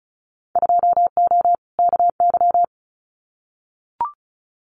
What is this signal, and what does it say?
Morse code, heard as a keyed tone, about 34 wpm, 700 Hz. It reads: 2O KY
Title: Winter Field Day Exchanges 35wpm